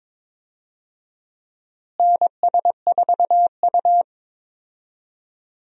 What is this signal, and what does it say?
Morse code, heard as a keyed tone, about 22 wpm, 700 Hz. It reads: NS4U
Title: USA Call Signs - 4 Characters - Repeated 3x - 22wpm